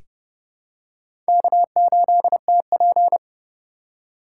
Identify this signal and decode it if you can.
Morse code, heard as a keyed tone, about 30 wpm, 700 Hz. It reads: K8TP
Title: USA Call Signs - 4 Characters - 4x Character Spacing - 30wpm